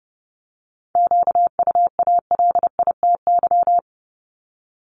Morse code, heard as a keyed tone, about 30 wpm, 690 Hz. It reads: QUALITY
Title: Multi-letter Phonemes 30wpm